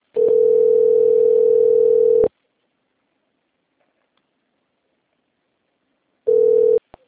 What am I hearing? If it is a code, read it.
Ringback tone (North America), 440 + 480 Hz.